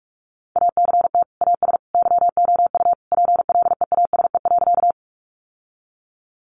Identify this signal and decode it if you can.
Morse code, heard as a keyed tone, about 44 wpm, 700 Hz. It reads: ACT AS YOU PLEASE.
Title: Sentences from Top 1500 Words 45wpm